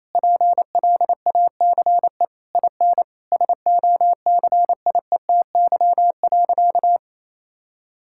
Morse code, heard as a keyed tone, about 28 wpm, 700 Hz. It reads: PLACE IN SOCIETY.
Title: Sentences from Top 1600 Words 28wpm